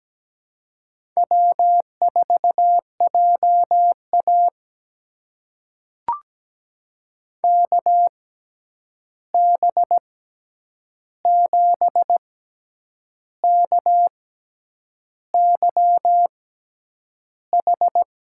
Morse code, heard as a keyed tone, about 17 wpm, spaced as for 8 wpm, 700 Hz. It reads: W4JA KB7KYH
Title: USA Call Signs - 6x Character Spacing - 17wpm